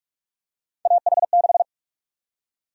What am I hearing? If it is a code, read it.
Morse code, heard as a keyed tone, about 46 wpm, 700 Hz. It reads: AH6
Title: USA Call Sign Prefixes - 45wpm